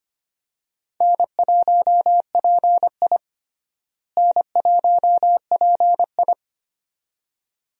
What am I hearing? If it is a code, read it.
Morse code, heard as a keyed tone, about 25 wpm, 700 Hz. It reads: N1PI N1PI
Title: USA Call Signs - 4 Characters - Repeated 2x - 25wpm